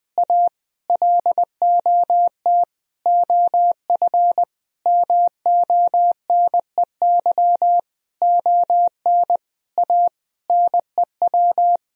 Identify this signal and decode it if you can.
Morse code, heard as a keyed tone, about 20 wpm, 700 Hz. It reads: A LOT OF MONEY ON A NEW
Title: Sentences from Top 1000 Words - Review Part 4 - 20wpm